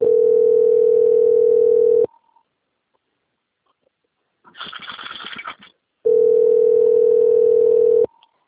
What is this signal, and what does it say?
Ringback tone (North America), 440 + 480 Hz.